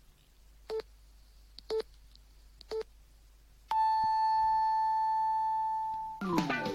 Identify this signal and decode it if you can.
Radio time signal (time pips). It is heard from 0.7 to 6.2 s.